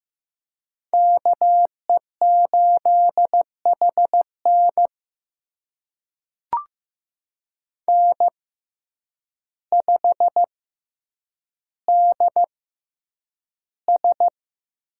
Morse code, heard as a keyed tone, about 15 wpm, spaced as for 8 wpm, 700 Hz. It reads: KE8HN N5DS
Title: USA Call Signs - 5 Characters - 6x Character Spacing - 15wpm